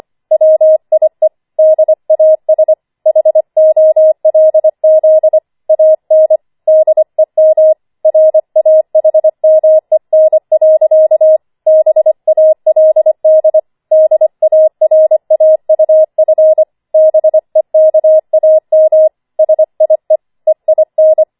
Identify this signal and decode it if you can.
Morse code, heard as a keyed tone, about 25 wpm, 630 Hz. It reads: WIE DAS HOLZ AN DEM RAHMEN. BALD DARAUF BEKAM SIE EIN